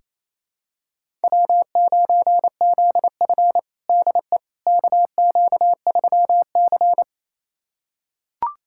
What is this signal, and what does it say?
Morse code, heard as a keyed tone, about 28 wpm, 700 Hz. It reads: W9ZF DE KQ3C
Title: USA Callsign DE Callsign - 4 Characters - 28wpm